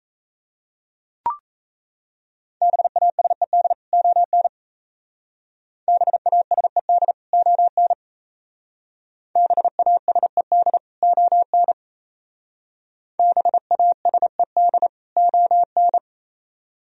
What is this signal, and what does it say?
Morse code, heard as a keyed tone, about 34 wpm, 700 Hz. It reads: BASED ON BASED ON BASED ON BASED ON
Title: Speed-Racing - Sets of 2 Words 28wpm